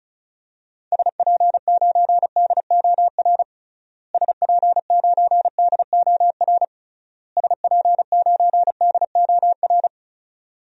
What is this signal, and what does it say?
Morse code, heard as a keyed tone, about 36 wpm, 700 Hz. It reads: SP9DOR SP9DOR SP9DOR
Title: DXCC-100 Call Signs - 6 Characters - Repeated 3x - 35wpm